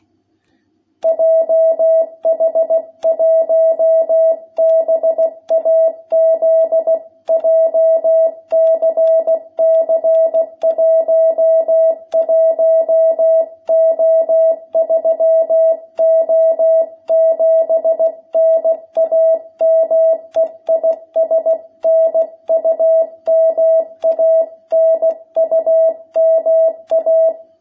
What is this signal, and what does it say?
Morse code, heard as a keyed tone, about 16 wpm, 660 Hz. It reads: JH1BAZ JCC 11O3O7 NAME IS NUMA NUMA
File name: jh8cbh / 2021cw